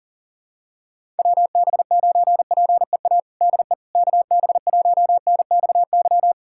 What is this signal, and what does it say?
Morse code, heard as a keyed tone, about 37 wpm, 700 Hz. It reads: WB9PEA DE KB1NXY
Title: USA Callsign DE Callsign - 6 Characters - 40wpm